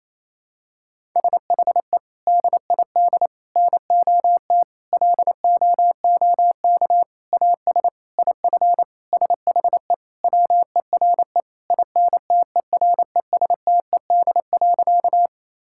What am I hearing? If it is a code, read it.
Morse code, heard as a keyed tone, about 28 wpm, 700 Hz. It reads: SHE DID NOT LOOK AS IF SHE WERE INTERESTED.
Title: Sentences from Top 1000 Words - Review Part 6 - 28wpm